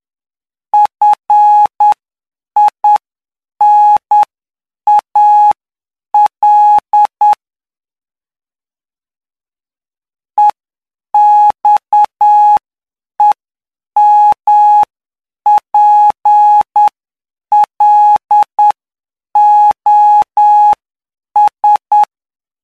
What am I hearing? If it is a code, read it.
Morse code, heard as a keyed tone, about 9 wpm, spaced as for 7 wpm, 840 Hz. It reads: FINAL EXEMPLOS
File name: cw / py2lcd / 11-PY2LCD-Li��o 20